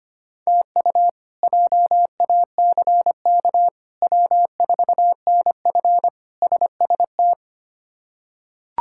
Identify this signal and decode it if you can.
Morse code, heard as a keyed tone, about 25 wpm, 700 Hz. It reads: TU JACK W4NF SST
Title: K1USN SST QSO Practice - 25wpm